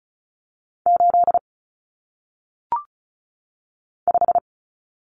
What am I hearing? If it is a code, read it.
Morse code, heard as a keyed tone, about 36 wpm, 700 Hz. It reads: 8 5
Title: Numbers - 1-Digit 35wpm